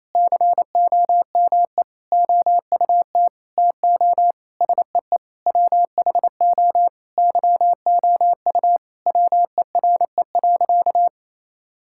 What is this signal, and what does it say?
Morse code, heard as a keyed tone, about 28 wpm, 700 Hz. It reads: COME OUT TO SEE WHO YOU WERE.
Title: Sentences from Top 1000 Words - Review Part 6 - 28wpm